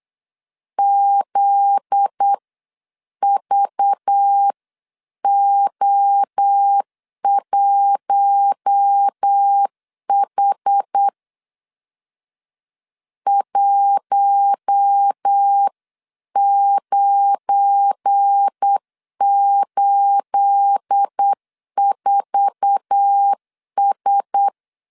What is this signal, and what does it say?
Morse code, heard as a keyed tone, about 8 wpm, 800 Hz. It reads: ZVO1H 1984S
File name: CW / DET / 1 9 1500C S C 8wpm 5653